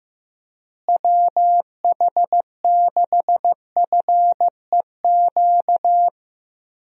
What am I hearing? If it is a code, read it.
Morse code, heard as a keyed tone, about 15 wpm, 700 Hz. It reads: WH6FEQ
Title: USA Call Signs - 6 Characters - Repeated 2x - 15wpm